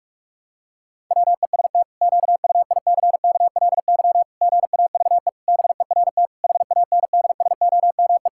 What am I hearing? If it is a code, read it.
Morse code, heard as a keyed tone, about 46 wpm, 700 Hz. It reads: WEST QUICKLY GAVE BERT HANDSOME
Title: Pangram Challenge v2 - 45wpm